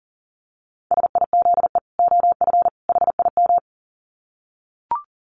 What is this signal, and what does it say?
Morse code, heard as a keyed tone, about 40 wpm, 700 Hz. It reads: SIZE OF HIM
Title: Dit Challenge 40wpm